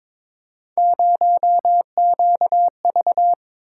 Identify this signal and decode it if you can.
Morse code, heard as a keyed tone, about 22 wpm, 700 Hz. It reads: 0QV
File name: practice / 22wpm / Three Letter-Number Combinations - 22wpm